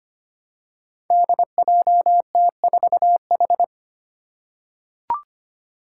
Morse code, heard as a keyed tone, about 25 wpm, 700 Hz. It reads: DJT4H
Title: Five Challenging Letter-Number Combinations 25wpm